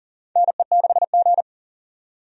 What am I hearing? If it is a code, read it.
Morse code, heard as a keyed tone, about 40 wpm, 700 Hz. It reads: NE6G
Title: USA Call Signs - 4 Characters - 40wpm